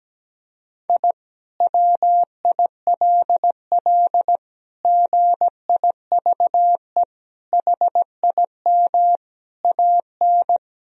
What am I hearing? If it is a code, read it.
Morse code, heard as a keyed tone, about 17 wpm, 700 Hz. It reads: I WILL GIVE HIM AN
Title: Sentences from Top 1000 Words - Review Part 4 - 17wpm